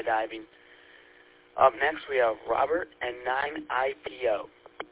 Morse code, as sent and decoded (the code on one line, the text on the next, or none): none